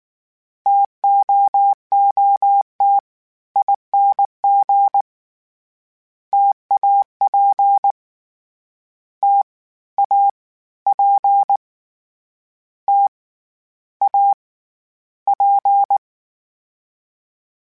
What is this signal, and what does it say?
Morse code, heard as a keyed tone, about 19 wpm, 800 Hz. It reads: TOOTING TAP T A P T A P
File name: Variable Speed / Variable Speed-mp3s / T AOEI's sp @18